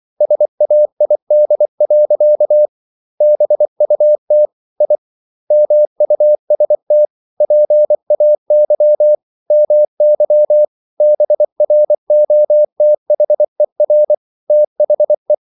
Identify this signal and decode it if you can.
Morse code, heard as a keyed tone, about 24 wpm, 600 Hz. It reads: SAID. BUT I MUST PAY MY BROTHER THE